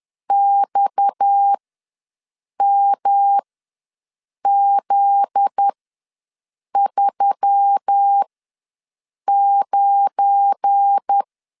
Morse code, heard as a keyed tone, about 10 wpm, spaced as for 6 wpm, 800 Hz. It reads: XMZ39
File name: LEZIONI / 7 GF9Z / 7 GF9Z ws 10wpm 3919